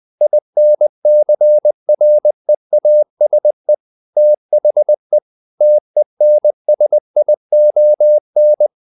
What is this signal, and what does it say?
Morse code, heard as a keyed tone, about 20 wpm, 600 Hz. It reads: INCREASE THE TENSION
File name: the old man and the sea / the old man and the sea part 1 20wpm